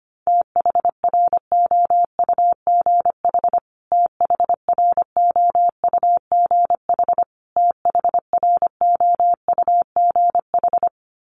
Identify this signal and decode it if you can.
Morse code, heard as a keyed tone, about 25 wpm, 700 Hz. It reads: THROUGH THROUGH THROUGH
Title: Top 300 Words Repeated 25wpm